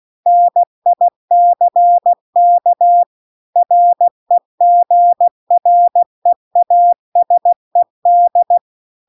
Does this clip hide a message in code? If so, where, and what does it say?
Yes: Morse code from 0.3 to 8.6 s, heard as a keyed tone, about 16 wpm, 700 Hz. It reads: NICK REGREASED